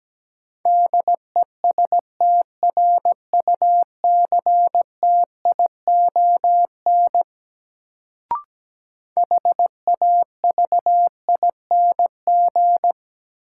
Morse code, heard as a keyed tone, about 17 wpm, 700 Hz. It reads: DESTRUCTION HAVING
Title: Sets of 5 Words Encore - New Word Drill - 17wpm